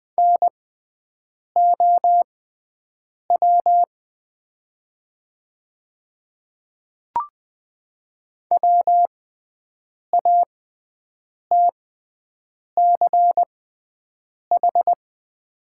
Morse code, heard as a keyed tone, about 20 wpm, spaced as for 7 wpm, 700 Hz. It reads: NOW WATCH
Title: 6x Character Spacing - Top 500 Words - 20wpm